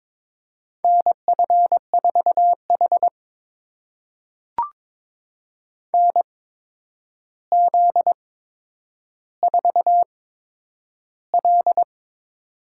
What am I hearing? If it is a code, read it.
Morse code, heard as a keyed tone, about 22 wpm, spaced as for 9 wpm, 700 Hz. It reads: NF4H NZ4L
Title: USA Call Signs - 4 Characters - 8x Character Spacing - 22wpm